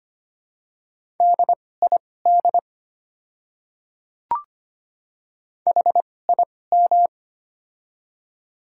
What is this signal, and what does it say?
Morse code, heard as a keyed tone, about 25 wpm, spaced as for 18 wpm, 700 Hz. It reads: DID HIM
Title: 2x Character Spacing - Top 500 Words - 25wpm